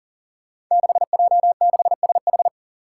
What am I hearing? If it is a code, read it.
Morse code, heard as a keyed tone, about 40 wpm, 700 Hz. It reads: 6J6SH
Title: Five Challenging Letter-Number Combinations 40wpm